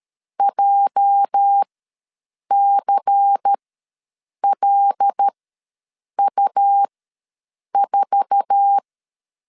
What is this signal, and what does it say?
Morse code, heard as a keyed tone, about 12 wpm, spaced as for 7 wpm, 800 Hz. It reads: JCLU4
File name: LEZIONI / 10 SIGNS / 10A SLASH ws 12wpm 3443